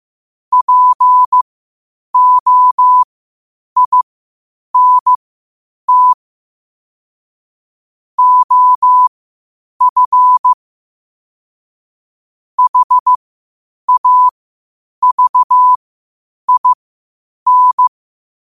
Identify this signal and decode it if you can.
Morse code, heard as a keyed tone, about 15 wpm, spaced as for 8 wpm, 1000 Hz. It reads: POINT OF HAVIN
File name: qotdfiles / 8wpm